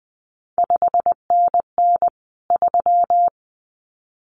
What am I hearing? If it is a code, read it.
Morse code, heard as a keyed tone, about 20 wpm, 700 Hz. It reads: 5NN 3
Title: CQ WW DX Contest Exchange Encore - 20wpm